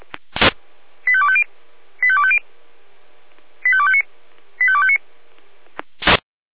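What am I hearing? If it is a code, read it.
Five-tone selective call (ZVEI), heard as quick runs of five tones, tones of about 70 ms. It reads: 85250 85250 85259 85259
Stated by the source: ZVEI five-tone selective call received with an SDR.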